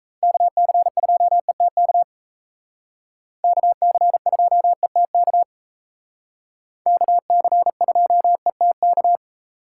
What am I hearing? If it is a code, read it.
Morse code, heard as a keyed tone, about 34 wpm, 700 Hz. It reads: KC2ETK KC2ETK KC2ETK
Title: Speed Racing - USA Call Signs 28wpm